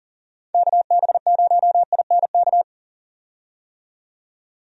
Morse code, heard as a keyed tone, about 40 wpm, 700 Hz. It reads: KB0INK
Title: USA Call Signs - 6 Characters - 40wpm